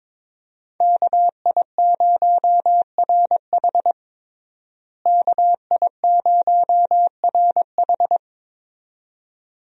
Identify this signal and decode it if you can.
Morse code, heard as a keyed tone, about 22 wpm, 700 Hz. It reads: KI0RH KI0RH
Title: USA Call Signs - 5 Characters - Repeated 2x - 22wpm